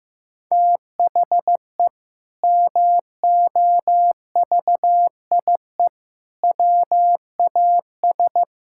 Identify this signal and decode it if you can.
Morse code, heard as a keyed tone, about 15 wpm, 700 Hz. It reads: THE MOVIE WAS
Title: Sentences from Top 1000 Words - Review Part 5 - 15wpm